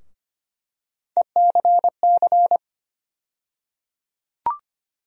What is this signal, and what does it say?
Morse code, heard as a keyed tone, about 25 wpm, 700 Hz. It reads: ECC
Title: USA Call Sign Postfixes - 3 Letters 25wpm